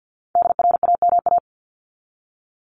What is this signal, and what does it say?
Morse code, heard as a keyed tone, about 50 wpm, 700 Hz. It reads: DRAMA
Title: Top 1251-1500 Words 50wpm